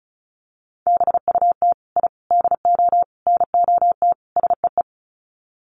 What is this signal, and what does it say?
Morse code, heard as a keyed tone, about 34 wpm, 700 Hz. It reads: BUT I DO NOT SEE
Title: Sets of 5 Words Encore 35wpm